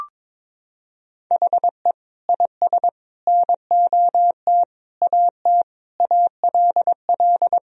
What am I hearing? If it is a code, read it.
Morse code, heard as a keyed tone, about 21 wpm, 700 Hz. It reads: HE IS NOT AT ALL
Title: Sentences from Top 1000 Words - Review Part 4 - 22wpm